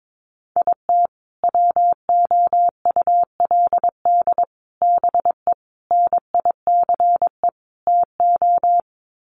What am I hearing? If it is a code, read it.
Morse code, heard as a keyed tone, about 22 wpm, 700 Hz. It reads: IT WOULD BE NICE TO
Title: Sets of 5 Words Encore2 22wpm